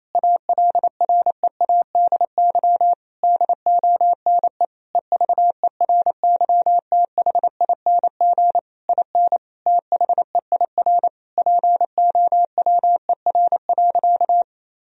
Morse code, heard as a keyed tone, about 28 wpm, 700 Hz. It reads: ALREADY DONE EVERYTHING IN THEIR POWER.
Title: Sentences from Top 1000 Words - Review Part 2 - 28wpm